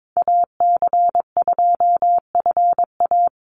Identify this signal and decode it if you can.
Morse code, heard as a keyed tone, about 22 wpm, 700 Hz. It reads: AC2FA
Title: Call Signs - 22wpm